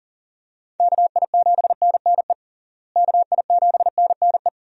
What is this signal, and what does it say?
Morse code, heard as a keyed tone, about 40 wpm, 700 Hz. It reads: KI7NNE KI7NNE
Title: USA Call Signs - 6 Characters - Repeated 2x - 40wpm